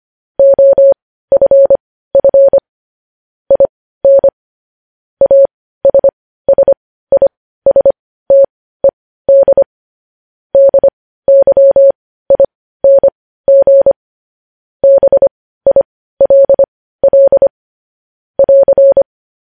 Morse code, heard as a keyed tone, about 25 wpm, spaced as for 15 wpm, 550 Hz. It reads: OFF IN ASSISTED DYING BILL <AR>
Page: news headlines in morse code at 15 wpm